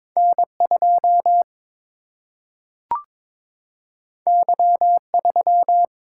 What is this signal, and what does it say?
Morse code, heard as a keyed tone, about 22 wpm, 700 Hz. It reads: N2 Y3
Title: Single-Letter Single-Number 22wpm